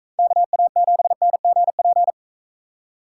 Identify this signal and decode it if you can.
Morse code, heard as a keyed tone, about 40 wpm, 700 Hz. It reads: KA7NGP